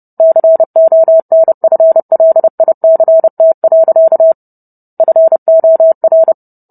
Morse code, heard as a keyed tone, about 30 wpm, 650 Hz. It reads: CONFLICT. FOR